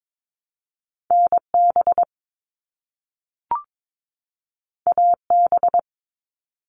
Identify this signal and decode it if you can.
Morse code, heard as a keyed tone, about 22 wpm, 700 Hz. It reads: NB AB
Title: Canadian Provinces and Territories 22wpm